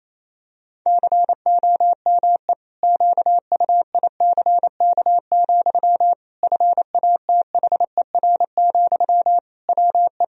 Morse code, heard as a keyed tone, about 27 wpm, 700 Hz. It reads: COME QUICK, FATHER, WE
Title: Sentences from Top 1000 Words - Review Part 3 - 28wpm